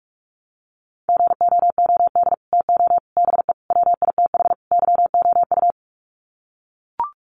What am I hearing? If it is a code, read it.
Morse code, heard as a keyed tone, about 44 wpm, 700 Hz. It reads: GOOD TO BE WITH YOU
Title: Sets of 5 Words Encore2 45wpm